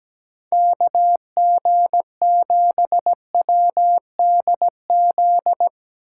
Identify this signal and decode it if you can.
Morse code, heard as a keyed tone, about 17 wpm, 700 Hz. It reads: KG7WDZ
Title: USA Call Signs - 6 Characters - 2x Character Spacing - 17wpm